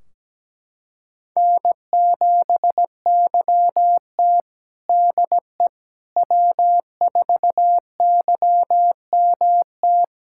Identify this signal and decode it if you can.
Morse code, heard as a keyed tone, about 17 wpm, 700 Hz. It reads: N7YT DE W4YMT
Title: K1USN SST QSO Practice 17wpm